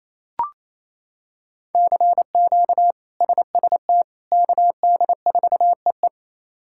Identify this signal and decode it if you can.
Morse code, heard as a keyed tone, about 28 wpm, 700 Hz. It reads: CQ SST KD4EE
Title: K1USN SST QSO Practice - 28wpm